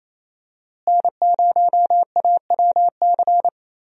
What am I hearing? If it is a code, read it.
Morse code, heard as a keyed tone, about 28 wpm, 700 Hz. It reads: N0AWC
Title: USA Call Signs - 5 Characters - Repeated 2x - 28wpm